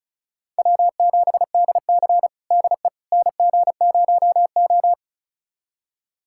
Morse code, heard as a keyed tone, about 36 wpm, 700 Hz. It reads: W7DC DE NG0O
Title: USA Callsign DE Callsign - 4 Characters - 35wpm